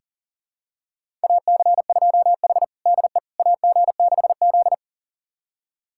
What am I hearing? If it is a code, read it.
Morse code, heard as a keyed tone, about 40 wpm, 700 Hz. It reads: AC2H DE AG6Z
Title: USA Callsign DE Callsign - 4 Characters - 40wpm